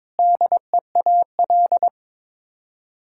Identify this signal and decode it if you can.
Morse code, heard as a keyed tone, about 22 wpm, 700 Hz. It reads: DEAL
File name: practice / 22wpm / Sets of 8 Words - 22wpm